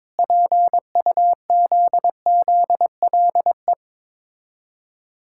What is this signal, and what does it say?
Morse code, heard as a keyed tone, about 23 wpm, 690 Hz. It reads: PUZZLE
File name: Difficult Words with Hints - 22wpm